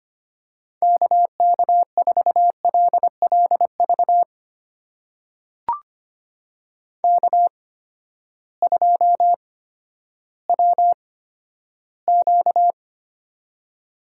Morse code, heard as a keyed tone, about 25 wpm, spaced as for 13 wpm, 700 Hz. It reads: KK4LLV K2WQ
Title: USA Call Signs - 8x Character Spacing - 25wpm